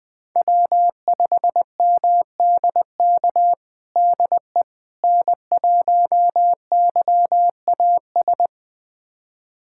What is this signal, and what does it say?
Morse code, heard as a keyed tone, about 20 wpm, 700 Hz. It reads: W5MDK DE N1YAS
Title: USA Callsign DE Callsign - 5 Characters - 20wpm